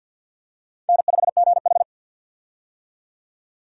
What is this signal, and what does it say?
Morse code, heard as a keyed tone, about 47 wpm, 700 Hz. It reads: N5GH
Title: USA Call Signs - 4 Characters - 50wpm